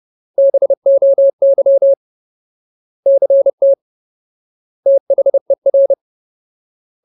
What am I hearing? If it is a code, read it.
Morse code, heard as a keyed tone, about 30 wpm, 550 Hz. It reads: BOY CT THER
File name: Ultimate Word List / 30WPM Ultimate